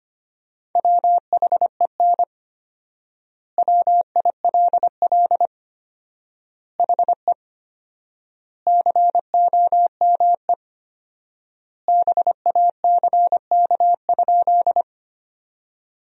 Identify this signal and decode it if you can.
Morse code, heard as a keyed tone, about 25 wpm, 700 Hz. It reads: WHEN WILL HE COME BACK?
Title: 4x Word-Spacing - Sentences from Top 200 Words - 25wpm